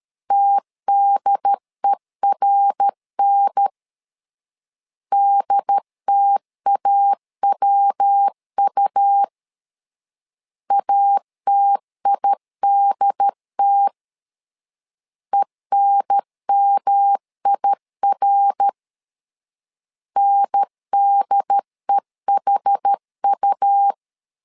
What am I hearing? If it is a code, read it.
Morse code, heard as a keyed tone, about 13 wpm, 800 Hz. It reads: TDERN DTAWU ATIDT ENMIR NDEHU
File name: LEZIONI / 4 Q 12wpm 1833